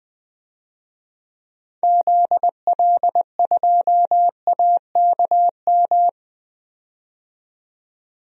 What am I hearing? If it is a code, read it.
Morse code, heard as a keyed tone, about 20 wpm, 700 Hz. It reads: ZL2AKM
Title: DXCC-100 Call Signs - 6 Characters - 20wpm